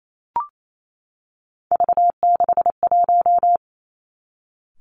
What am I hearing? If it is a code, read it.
Morse code, heard as a keyed tone, about 28 wpm, 700 Hz. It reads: V61
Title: Three Challenging Letter-Number Combinations 28wpm